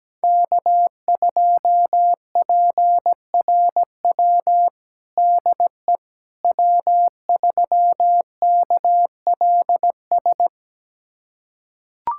Morse code, heard as a keyed tone, about 17 wpm, 700 Hz. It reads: K2PRW DE W3KLS
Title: USA Callsign DE Callsign - 5 Characters - 17wpm